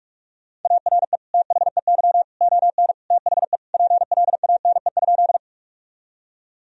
Morse code, heard as a keyed tone, about 46 wpm, 700 Hz. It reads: ARE THEY ON THE PLANE?
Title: Sentences from Top 1000 Words - Review Part 3 - 45wpm